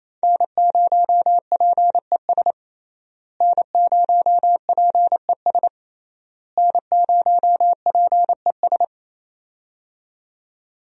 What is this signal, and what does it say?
Morse code, heard as a keyed tone, about 28 wpm, 700 Hz. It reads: N0PES N0PES N0PES
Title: USA Call Signs - 5 Characters - Repeated 3x - 28wpm